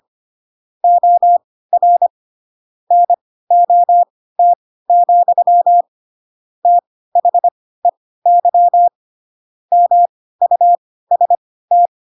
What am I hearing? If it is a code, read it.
Morse code, heard as a keyed tone, about 25 wpm, spaced as for 16 wpm, 700 Hz. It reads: OR NOT, THEY MUST
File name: Pinocchio / Pinocchio ch4 10 25 ik1hge